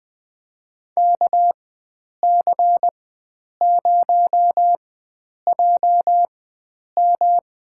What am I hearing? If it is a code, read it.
Morse code, heard as a keyed tone, about 20 wpm, spaced as for 9 wpm, 700 Hz. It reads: KC0JM
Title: USA Call Signs - 4x Character Spacing - 20wpm